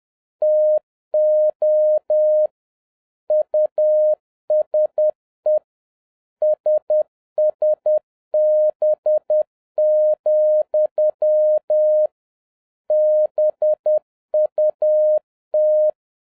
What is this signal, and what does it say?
Morse code, heard as a keyed tone, about 10 wpm, 610 Hz. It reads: TO USE SSB, BUT